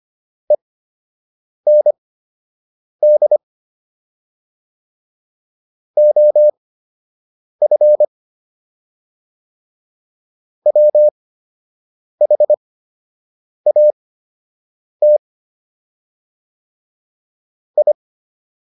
Morse code, heard as a keyed tone, about 25 wpm, spaced as for 7 wpm, 600 Hz. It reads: END OF WHAT I